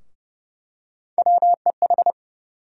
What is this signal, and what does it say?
Morse code, heard as a keyed tone, about 30 wpm, 700 Hz. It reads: WEH
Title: USA Call Sign Postfixes - 3 Letters 30wpm